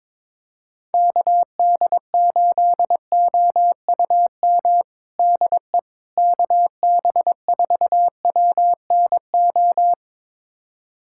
Morse code, heard as a keyed tone, about 21 wpm, 700 Hz. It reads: KD8OUM DE KB4WNO
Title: USA Callsign DE Callsign 22wpm